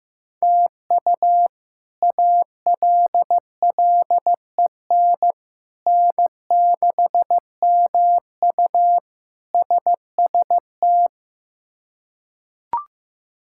Morse code, heard as a keyed tone, about 15 wpm, 700 Hz. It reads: TU ALLEN N6MU SST
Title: K1USN SST QSO Practice - 15wpm